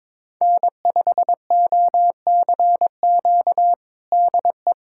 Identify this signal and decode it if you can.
Morse code, heard as a keyed tone, about 22 wpm, 700 Hz. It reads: N5OCQ DE
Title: USA Callsign DE Callsign - 5 Characters - 22wpm